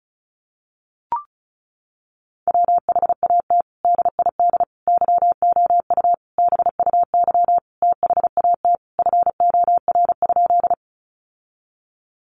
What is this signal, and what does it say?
Morse code, heard as a keyed tone, about 34 wpm, 700 Hz. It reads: WHAT DID YOU BUY THAT FOR?
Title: Sentences from Top 700 Words 35wpm